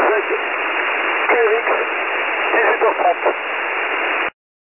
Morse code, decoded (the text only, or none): none